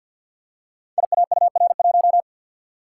Morse code, heard as a keyed tone, about 51 wpm, 700 Hz. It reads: IAUR1
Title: Code Groups - 5 Letter:Numbers - 50wpm